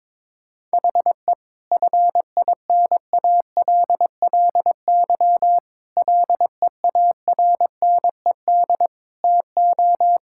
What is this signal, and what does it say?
Morse code, heard as a keyed tone, about 22 wpm, 700 Hz. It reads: HE FINALLY LEARNED TO
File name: Difficult Sentences with Hints - 22wpm